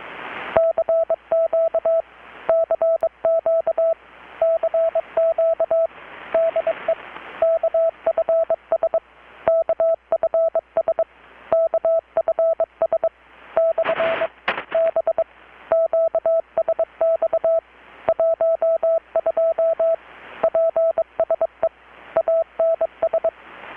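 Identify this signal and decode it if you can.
Morse code, heard as a keyed tone, about 22 wpm, 660 Hz, with radio noise. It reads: CQ CQ CQ DE KFS KFS KFS / B QSX 12 PSE ANS
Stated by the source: Morse from KPH's Night of Nights XX maritime radiotelegraphy event, received in Chiapas, Mexico about 3,800 km away on a Kenwood R-600 with a grounded 10 m long-wire antenna.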